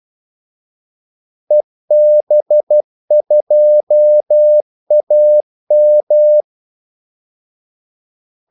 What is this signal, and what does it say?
Morse code, heard as a keyed tone, about 12 wpm, 600 Hz. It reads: EB2AM
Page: 2 ekstra ordmellemrum mellem kaldesignalerne